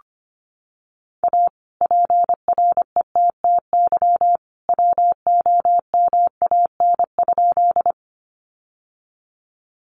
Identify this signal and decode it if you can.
Morse code, heard as a keyed tone, about 25 wpm, 700 Hz. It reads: A PRETTY WOMAN?
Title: Sentences from Top 1100 Words 25wpm